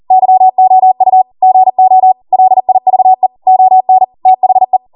Morse code, heard as a keyed tone, about 40 wpm, 750 Hz. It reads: YOU GO LIVE ON THE